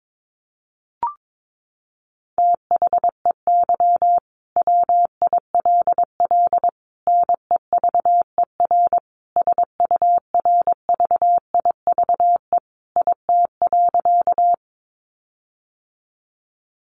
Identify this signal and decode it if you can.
Morse code, heard as a keyed tone, about 22 wpm, 700 Hz. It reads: THEY WILL NEVER SURVIVE IT.
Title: Sentences from Top 1300 Words 22wpm